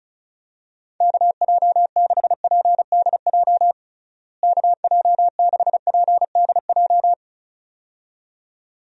Morse code, heard as a keyed tone, about 36 wpm, 700 Hz. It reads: KJ6PDJ KJ6PDJ
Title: USA Call Signs - 6 Characters - Repeated 2x - 35wpm